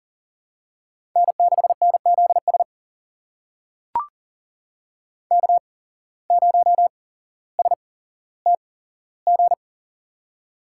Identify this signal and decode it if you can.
Morse code, heard as a keyed tone, about 40 wpm, spaced as for 17 wpm, 700 Hz. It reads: N6NZS K0STG
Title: USA Call Signs - 5 Characters - 8x Character Spacing - 40wpm